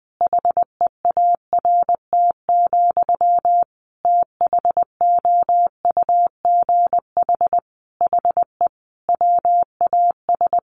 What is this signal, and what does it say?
Morse code, heard as a keyed tone, about 20 wpm, 700 Hz. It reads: HEART, THOUGH HE WAS
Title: Sentences from Top 1400 Words 20wpm